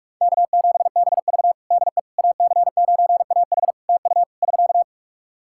Morse code, heard as a keyed tone, about 44 wpm, 700 Hz. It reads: K7BV DE AC9AH TU <SK>
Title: Practice QSO - Exchange 4 - Ending - 45wpm